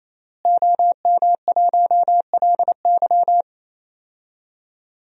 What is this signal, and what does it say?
Morse code, heard as a keyed tone, about 28 wpm, 700 Hz. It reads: OM1LY
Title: DXCC-100 Call Signs - 5 Characters - Repeated 3x - 28wpm